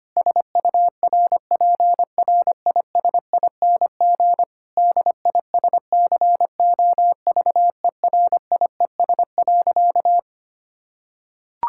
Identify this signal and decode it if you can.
Morse code, heard as a keyed tone, about 25 wpm, 700 Hz. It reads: SURPRISING DISCOVERIES.
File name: practice / 25wpm / Difficult Sentences with Hints - 25wpm